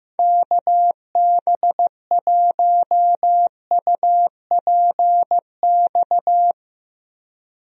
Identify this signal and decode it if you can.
Morse code, heard as a keyed tone, about 15 wpm, 700 Hz. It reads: KB1UPX
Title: USA Call Signs - 6 Characters - Repeated 3x - 15wpm